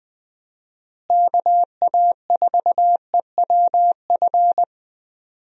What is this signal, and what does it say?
Morse code, heard as a keyed tone, about 20 wpm, 700 Hz. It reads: KA4EWF
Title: USA Call Signs - 6 Characters - Repeated 3x - 20wpm